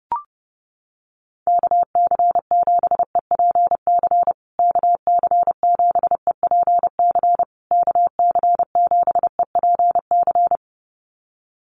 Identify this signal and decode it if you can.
Morse code, heard as a keyed tone, about 30 wpm, 700 Hz. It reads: KC7EPC KC7EPC KC7EPC
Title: USA Call Signs Repeated 3x 30wpm